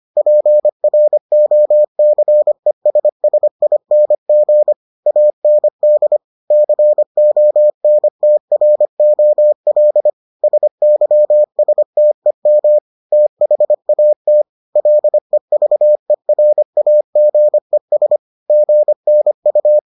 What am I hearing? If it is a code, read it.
Morse code, heard as a keyed tone, about 25 wpm, 600 Hz. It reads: PROCESSING AND CONTROL SYSTEM THAT LEVERAGES GNU